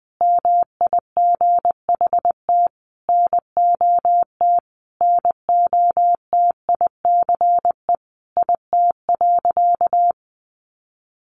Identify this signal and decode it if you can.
Morse code, heard as a keyed tone, about 20 wpm, 700 Hz. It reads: MIGHT NOT NOTICE IT.
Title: Sentences from Top 1100 Words 20wpm